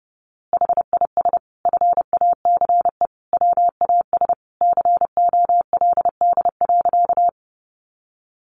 Morse code, heard as a keyed tone, about 28 wpm, 700 Hz. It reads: HIS FACE WAS COLD.
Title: A Sound Of Thunder 30wpm